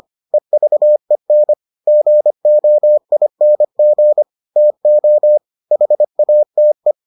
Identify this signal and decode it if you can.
Morse code, heard as a keyed tone, about 25 wpm, 600 Hz. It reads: EVEN GOING TO HATE